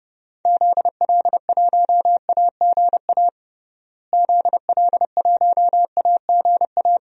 Morse code, heard as a keyed tone, about 32 wpm, 700 Hz. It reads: ZL1AGA ZL1AGA
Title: DXCC-100 Call Signs - 6 Characters - Repeated 3x - 30wpm